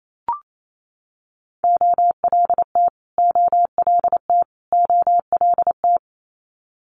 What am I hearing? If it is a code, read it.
Morse code, heard as a keyed tone, about 28 wpm, 700 Hz. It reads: OLT OLT OLT
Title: Multi-letter Phonemes Repeated 3x 28wpm